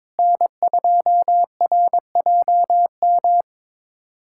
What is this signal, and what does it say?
Morse code, heard as a keyed tone, about 22 wpm, 700 Hz. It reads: N2RJM
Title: USA Callsign DE Callsign - 5 Characters - 22wpm